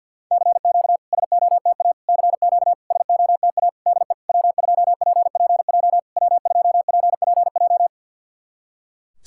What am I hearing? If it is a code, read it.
Morse code, heard as a keyed tone, about 53 wpm, 700 Hz. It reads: CQ SOTA CQ SOTA DE W2PPJ W2PPJ
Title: SOTA QSO Practice 50wpm